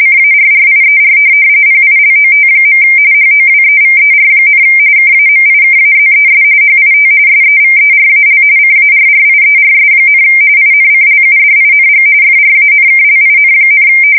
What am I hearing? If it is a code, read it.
RTTY, about 45.45 baud. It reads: CARRIER PILOT. HOWEVER, KAMIKAZES HIT SEVERAL CARRIERS, INCLUDING FRANKLIN,
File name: fsk170 / USS-New-Jersey